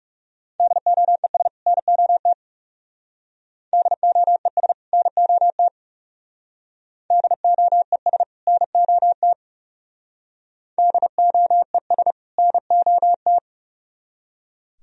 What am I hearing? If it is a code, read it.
Morse code, heard as a keyed tone, about 38 wpm, 700 Hz. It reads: DOES NOT DOES NOT DOES NOT DOES NOT